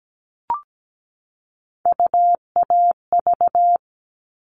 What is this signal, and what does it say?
Morse code, heard as a keyed tone, about 17 wpm, 700 Hz. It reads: UAV
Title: All Cut Numbers - 3-Digit 17wpm